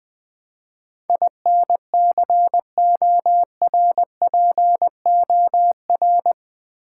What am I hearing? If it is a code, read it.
Morse code, heard as a keyed tone, about 20 wpm, 700 Hz. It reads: INCORPOR
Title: Build A Word - 4 Letters At A Time - 11-Letter Words - 20wpm